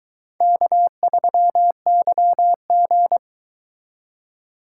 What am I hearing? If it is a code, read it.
Morse code, heard as a keyed tone, about 22 wpm, 700 Hz. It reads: K3YG